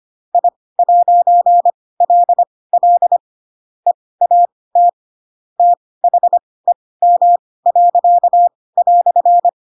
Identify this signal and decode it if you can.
Morse code, heard as a keyed tone, about 25 wpm, spaced as for 18 wpm, 700 Hz. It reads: I'LL EAT THEM."
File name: Pinocchio / Pinocchio ch7 12 25 ik1hge